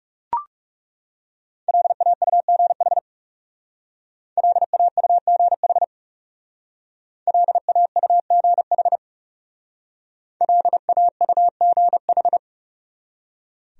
Morse code, heard as a keyed tone, about 35 wpm, 700 Hz. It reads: LAUGH LAUGH LAUGH LAUGH
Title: Speed-Racing Top 500 Words 30wpm